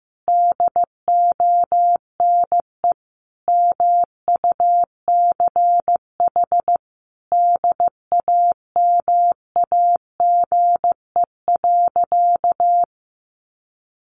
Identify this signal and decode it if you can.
Morse code, heard as a keyed tone, about 15 wpm, 700 Hz. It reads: DONE MUCH DAMAGE.
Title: Sentences from Top 1100 Words 15wpm